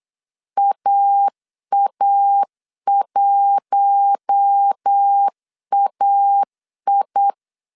Morse code, heard as a keyed tone, about 8 wpm, 800 Hz. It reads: AA1AI
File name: CW / DET / 1 9 1500C S C 8wpm 5653